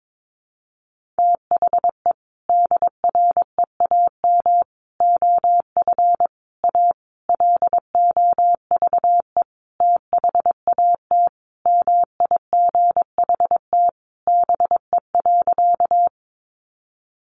Morse code, heard as a keyed tone, about 22 wpm, 700 Hz. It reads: THE DREAM OF A LOVE THAT MIGHT BE.
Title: Sentences from Top 1500 Words 22wpm